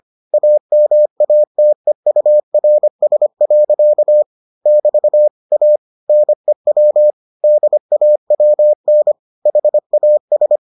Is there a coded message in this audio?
Yes: Morse code from 0.3 to 10.6 s, heard as a keyed tone, about 25 wpm, 600 Hz. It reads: AMATEURS. <BT> A NEW DAWN HAS